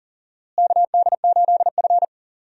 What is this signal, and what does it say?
Morse code, heard as a keyed tone, about 40 wpm, 700 Hz. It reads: KD8F
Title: USA Call Signs - 4 Characters - Repeated 2x - 40wpm